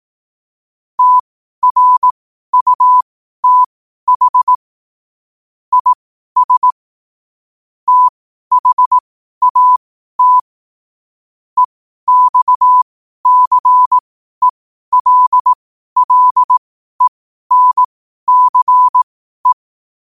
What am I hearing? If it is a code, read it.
Morse code, heard as a keyed tone, about 18 wpm, spaced as for 12 wpm, 1000 Hz. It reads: TRUTH IS THAT EXCELLENCE
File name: qotdfiles / 12wpm